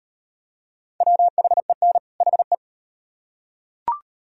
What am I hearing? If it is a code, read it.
Morse code, heard as a keyed tone, about 34 wpm, 700 Hz. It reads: WHEN HE